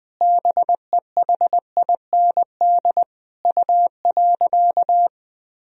Morse code, heard as a keyed tone, about 20 wpm, 700 Hz. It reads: BEHIND U.
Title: Sentences from Top 500 Words and Abbreviations 20wpm